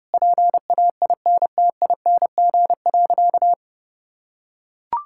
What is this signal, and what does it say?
Morse code, heard as a keyed tone, about 30 wpm, 700 Hz. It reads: PAINTING.
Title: Sentences from Top 1000 Words - Review Part 6 - 30wpm